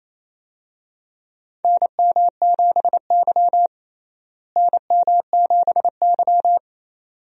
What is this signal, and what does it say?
Morse code, heard as a keyed tone, about 28 wpm, 700 Hz. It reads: NM7Y NM7Y
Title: USA Call Signs - 4 Characters - Repeated 2x - 28wpm